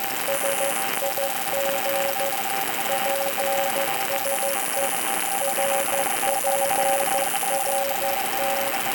Morse code, heard as a keyed tone, about 14 wpm, 550 Hz, with noise. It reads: SIG PSE RPRT